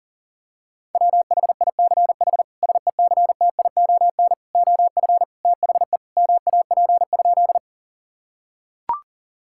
Morse code, heard as a keyed tone, about 40 wpm, 700 Hz. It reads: WHICH SECTION OF THE MAP?
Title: Sentences from Top 1000 Words - Review Part 5 - 40wpm